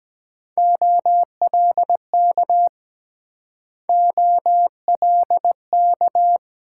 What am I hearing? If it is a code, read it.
Morse code, heard as a keyed tone, about 22 wpm, 700 Hz. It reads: OLK OLK